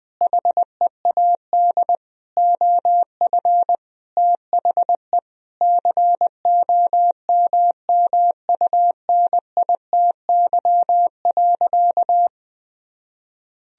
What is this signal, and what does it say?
Morse code, heard as a keyed tone, about 20 wpm, 700 Hz. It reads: HEAD OF THE COMMUNITY.
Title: Sentences from Top 1000 Words - Review Part 4 - 20wpm